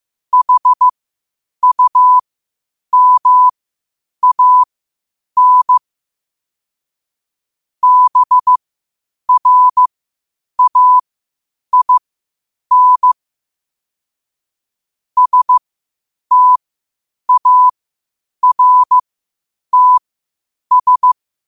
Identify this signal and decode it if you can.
Morse code, heard as a keyed tone, about 15 wpm, spaced as for 8 wpm, 1000 Hz. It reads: HUMAN BRAIN STARTS
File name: qotdfiles / 8wpm